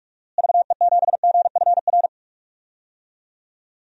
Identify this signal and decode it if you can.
Morse code, heard as a keyed tone, about 44 wpm, 700 Hz. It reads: VE7GFR
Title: DXCC-100 Call Signs - 6 Characters - Repeated 3x - 45wpm